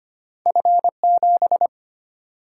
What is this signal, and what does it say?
Morse code, heard as a keyed tone, about 34 wpm, spaced as for 30 wpm, 700 Hz. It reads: F7
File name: practice / 25wpm / Speed-Racing - Two Letter-Number Combinations - 25wpm